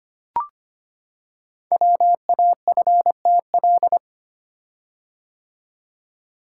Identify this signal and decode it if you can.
Morse code, heard as a keyed tone, about 25 wpm, 700 Hz. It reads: WAFTL
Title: Code Groups - 5 Letters - 25wpm